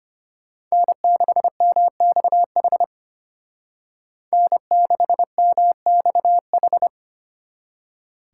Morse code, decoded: N6MXH N6MXH

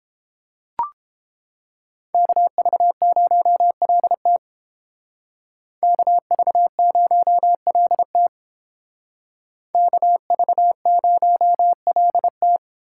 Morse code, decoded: KV0LT KV0LT KV0LT

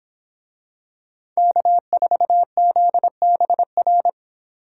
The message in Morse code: K4ZBR

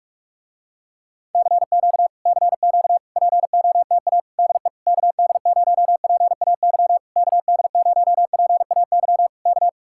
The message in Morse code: CQ CQ POTA DE KD0PAY KD0PAY K

